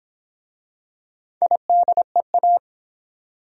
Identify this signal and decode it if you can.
Morse code, heard as a keyed tone, about 26 wpm, 700 Hz. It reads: IDEA